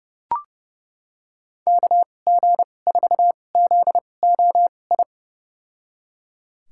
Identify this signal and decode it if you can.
Morse code, heard as a keyed tone, about 30 wpm, spaced as for 22 wpm, 700 Hz. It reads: KG4ZOI
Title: USA Call Signs - 6 Characters - 2x Character Spacing - 30wpm